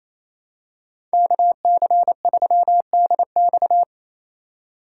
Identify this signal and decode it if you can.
Morse code, heard as a keyed tone, about 28 wpm, 700 Hz. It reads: KC3DX
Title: USA Call Signs - 5 Characters - 2x Character Spacing - 28wpm